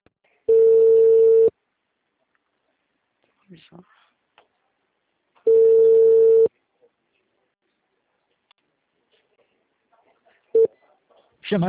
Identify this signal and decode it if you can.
Ringback tone (China, 450 Hz), with speech.